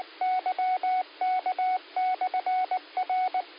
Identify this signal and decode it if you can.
Morse code, heard as a keyed tone, about 19 wpm, 740 Hz, with radio noise. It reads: YK/R
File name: qsorec SimplexLogic 16-44-13